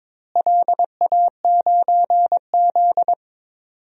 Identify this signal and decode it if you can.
Morse code, heard as a keyed tone, about 21 wpm, 700 Hz. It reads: LA9Z
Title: DXCC-100 Call Signs - 4 Characters - Repeated 2x - 22wpm